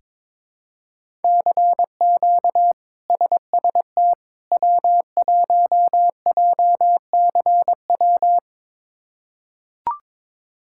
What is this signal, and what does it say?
Morse code, heard as a keyed tone, about 22 wpm, 700 Hz. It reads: CQ SST W1JCW
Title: K1USN SST QSO Practice - 22wpm